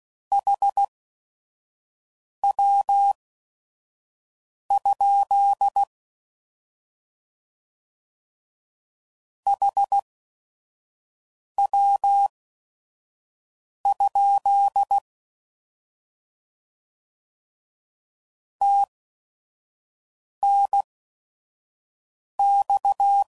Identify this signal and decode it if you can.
Morse code, heard as a keyed tone, about 16 wpm, spaced as for 5 wpm, 800 Hz. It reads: HW? HW? TNX
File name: Mixed / Abreviations / Abreviations